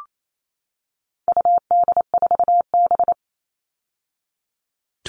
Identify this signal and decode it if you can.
Morse code, heard as a keyed tone, about 28 wpm, 700 Hz. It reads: UD4B
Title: All Cut Numbers - 4-Digit 28wpm